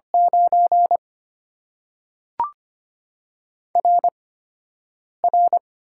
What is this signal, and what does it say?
Morse code, heard as a keyed tone, about 25 wpm, spaced as for 7 wpm, 700 Hz. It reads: 9 RR
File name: Top 90 QSO Elements - 8x Character Spacing - 25wpm